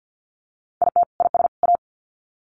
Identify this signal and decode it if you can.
Morse code, heard as a keyed tone, about 50 wpm, 700 Hz. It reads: IT IS A
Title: Sets of 3 Words 50wpm